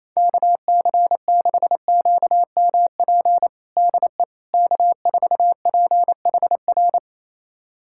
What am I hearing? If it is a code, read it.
Morse code, heard as a keyed tone, about 28 wpm, 700 Hz. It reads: KC6QMP DE K4PHR
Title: USA Callsign DE Callsign 28wpm